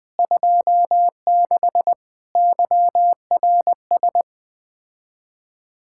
Morse code, heard as a keyed tone, about 20 wpm, 700 Hz. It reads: 26 YRS
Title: Speed-Racing Common QSO Phrases - 17wpm